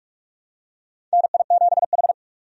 Morse code, heard as a keyed tone, about 43 wpm, 700 Hz. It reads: NI7H
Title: USA Call Signs - 4 Characters - 45wpm